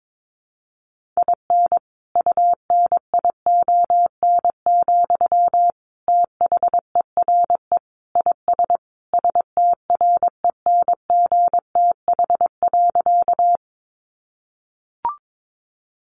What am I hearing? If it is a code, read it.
Morse code, heard as a keyed tone, about 22 wpm, 700 Hz. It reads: IN UNION, THERE IS STRENGTH.
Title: Sentences from Top 1400 Words 22wpm